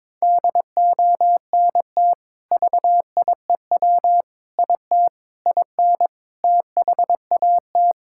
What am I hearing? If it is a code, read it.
Morse code, heard as a keyed tone, about 22 wpm, 700 Hz. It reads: DONT VIEW IT IN THAT
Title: Sentences from Top 1000 Words - Review Part 6 - 22wpm